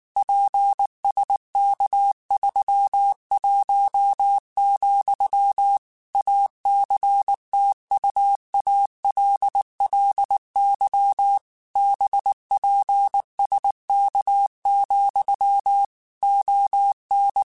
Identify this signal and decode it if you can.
Morse code, heard as a keyed tone, about 19 wpm, 800 Hz. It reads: PSK31, ACTUALLY BPSK, ON
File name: QSOs / QSOs-mp3s / WA7QOX 2@18